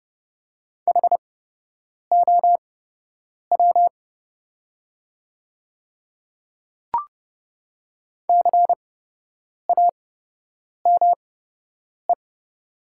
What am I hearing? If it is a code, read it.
Morse code, heard as a keyed tone, about 31 wpm, spaced as for 8 wpm, 700 Hz. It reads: HOW CAME
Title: 8x Character Spacing - Top 300 Words - 30wpm